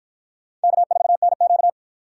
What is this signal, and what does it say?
Morse code, heard as a keyed tone, about 53 wpm, 700 Hz. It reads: K4NQ